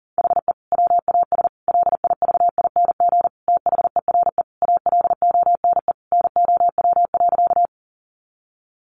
Morse code, heard as a keyed tone, about 37 wpm, 700 Hz. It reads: HE WAS LIVING THERE ALONE NOW.
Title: Sentences from Top 1100 Words 40wpm